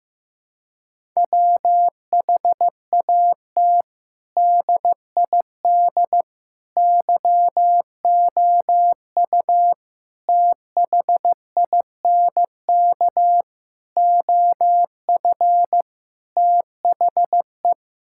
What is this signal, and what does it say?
Morse code, heard as a keyed tone, about 15 wpm, 700 Hz. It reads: WHAT DID YOU THINK OF THE
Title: Sentences from Top 1000 Words - Review Part 5 - 15wpm